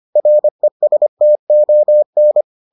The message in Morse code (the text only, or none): RESTON